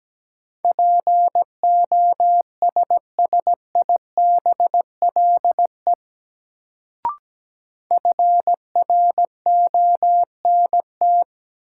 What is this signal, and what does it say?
Morse code, heard as a keyed tone, about 17 wpm, 700 Hz. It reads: POSSIBLE FRONT
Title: Sentences from Top 500 Words - New Word Drill - 17wpm